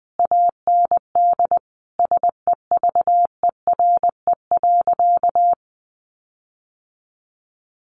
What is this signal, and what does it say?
Morse code, heard as a keyed tone, about 20 wpm, 700 Hz. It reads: AND SEVERE.
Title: Sentences from Top 1400 Words 20wpm